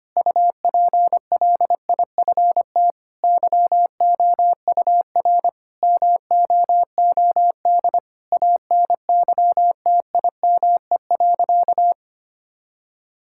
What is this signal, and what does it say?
Morse code, heard as a keyed tone, about 25 wpm, 700 Hz. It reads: UPLIFT YOUR MOOD ANYTIME.
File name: practice / 25wpm / Difficult Sentences with Hints - 25wpm